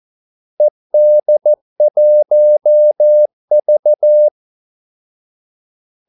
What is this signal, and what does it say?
Morse code, heard as a keyed tone, about 14 wpm, 600 Hz. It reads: ED1V
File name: Callsigns-14wpm